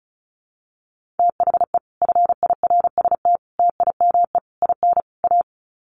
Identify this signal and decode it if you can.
Morse code, heard as a keyed tone, about 34 wpm, 700 Hz. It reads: THE FIRST TIME IN A
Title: Sets of 5 Words Encore 35wpm